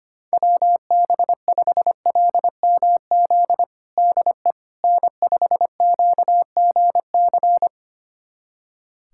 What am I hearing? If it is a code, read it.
Morse code, heard as a keyed tone, about 25 wpm, 700 Hz. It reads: WB5LMZ DE N5QGC
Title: USA Callsign DE Callsign 25wpm